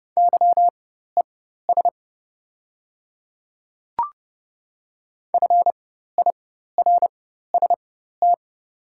Morse code, heard as a keyed tone, about 30 wpm, spaced as for 14 wpm, 700 Hz. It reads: YES FIRST
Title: 4x Character Spacing - Top 500 Words - 30wpm